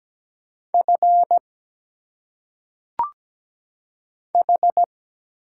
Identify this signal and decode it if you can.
Morse code, heard as a keyed tone, about 17 wpm, 700 Hz. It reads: F H
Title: Single Letters - Phonetic Pronunciation 17wpm